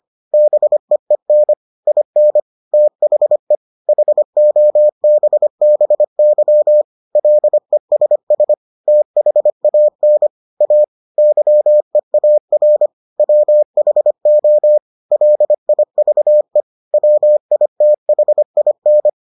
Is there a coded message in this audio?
Yes: Morse code from 0.3 to 19.1 s, heard as a keyed tone, about 25 wpm, 600 Hz. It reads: BEEN IN THE HOBBY LESS THAN A YEAR WHO LIVE WITHIN